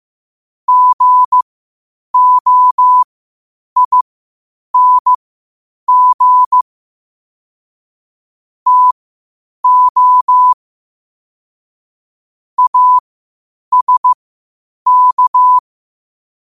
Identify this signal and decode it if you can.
Morse code, heard as a keyed tone, about 15 wpm, spaced as for 8 wpm, 1000 Hz. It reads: GOING TO ASK
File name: qotdfiles / 8wpm